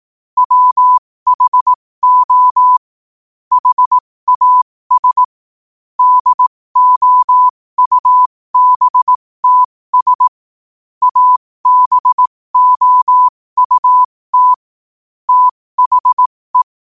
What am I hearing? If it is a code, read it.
Morse code, heard as a keyed tone, about 18 wpm, spaced as for 15 wpm, 1000 Hz. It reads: WHO HAS DOUBTS ABOUT THE
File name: qotdfiles / 15wpm